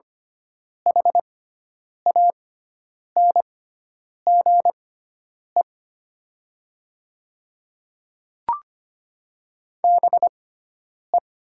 Morse code, heard as a keyed tone, about 25 wpm, spaced as for 9 wpm, 690 Hz. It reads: HANGE BE
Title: 6x Character Spacing - Top 200 Words - 25wpm